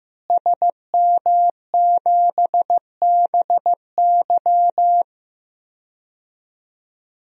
Morse code, heard as a keyed tone, about 16 wpm, 700 Hz. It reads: SM7BY
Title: DXCC-100 Call Signs - 5 Characters - Repeated 2x - 15wpm